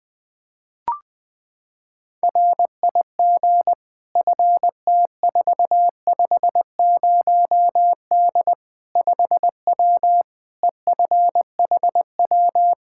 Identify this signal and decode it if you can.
Morse code, heard as a keyed tone, about 20 wpm, 700 Hz. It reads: RIG FT450D 5W EFHW
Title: Practice QSO - Exchange 2 - Info - 20wpm